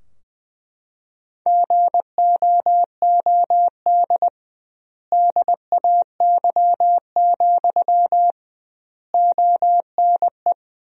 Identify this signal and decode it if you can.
Morse code, heard as a keyed tone, about 20 wpm, 700 Hz. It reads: GOOD DAY, ONE
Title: 2x Word-Spacing - Sentences from Top 200 Words - 20wpm